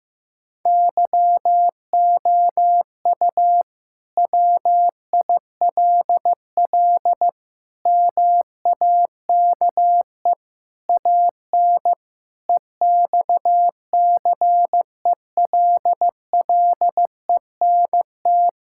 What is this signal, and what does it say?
Morse code, heard as a keyed tone, about 15 wpm, 700 Hz. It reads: YOU WILL MAKE AN EXCELLENT
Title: Sentences from Top 1000 Words - Review Part 5 - 15wpm